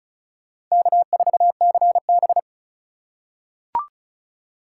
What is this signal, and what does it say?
Morse code, heard as a keyed tone, about 36 wpm, 700 Hz. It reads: K4CB
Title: USA Call Signs - 4 Characters - 4x Character Spacing - 35wpm